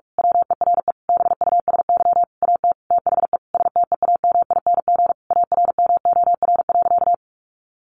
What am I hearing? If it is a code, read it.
Morse code, heard as a keyed tone, about 44 wpm, 700 Hz. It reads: WERE BUSY AT THE STEAMING ARMOR.
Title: A Sound Of Thunder 45wpm